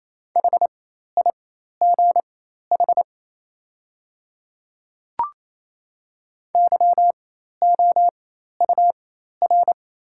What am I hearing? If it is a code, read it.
Morse code, heard as a keyed tone, about 27 wpm, spaced as for 13 wpm, 700 Hz. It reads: HIGH YOUR
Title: 4x Character Spacing - Top 300 Words - 28wpm